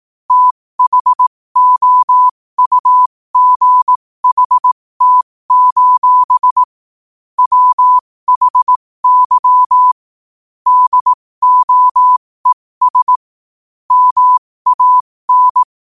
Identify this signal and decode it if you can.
Morse code, heard as a keyed tone, about 18 wpm, spaced as for 15 wpm, 1000 Hz. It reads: THOUGHT: WHY DOES MAN
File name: qotdfiles / 15wpm